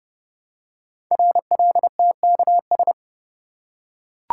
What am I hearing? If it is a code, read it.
Morse code, heard as a keyed tone, about 30 wpm, 700 Hz. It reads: RLTKS
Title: Code Groups - 5 Letters - 30wpm